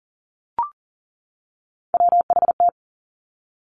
Morse code, heard as a keyed tone, about 41 wpm, 700 Hz. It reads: WHT
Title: Three Challenging Letter-Number Combinations 40wpm